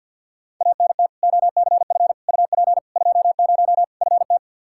Morse code, heard as a keyed tone, about 46 wpm, 700 Hz. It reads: ANT OCF UP 20 FT
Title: K1USN SST QSO Practice 50wpm